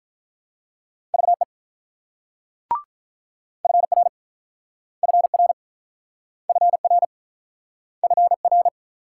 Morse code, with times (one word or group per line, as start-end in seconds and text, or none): 1.1-1.4 VE
3.6-4.1 FR
5.0-5.5 FR
6.5-7.0 FR
8.0-8.7 FR